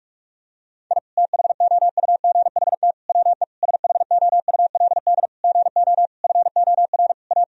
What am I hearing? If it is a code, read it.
Morse code, heard as a keyed tone, about 42 wpm, 700 Hz. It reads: I THOUGHT WE SHOULD GO FOR A
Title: Sentences from Top 1000 Words - Review Part 2 - 45wpm